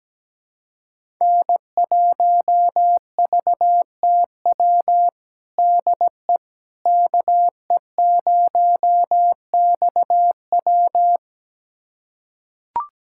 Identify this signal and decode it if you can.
Morse code, heard as a keyed tone, about 17 wpm, 700 Hz. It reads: N1VTW DE KE0XW
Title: USA Callsign DE Callsign - 5 Characters - 17wpm